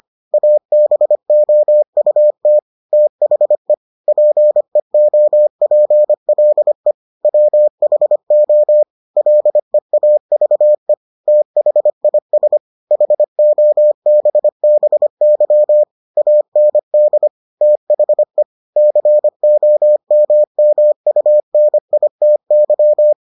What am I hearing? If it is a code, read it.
Morse code, heard as a keyed tone, about 25 wpm, 600 Hz. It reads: ABOUT THE PEOPLE WHO LEAVE THIS HOBBY AND THE COMMUNITY